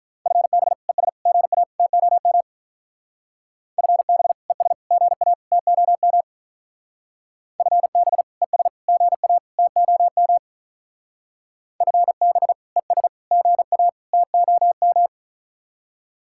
Morse code, heard as a keyed tone, about 45 wpm, 700 Hz. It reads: FB ES GA TOM FB ES GA TOM FB ES GA TOM FB ES GA TOM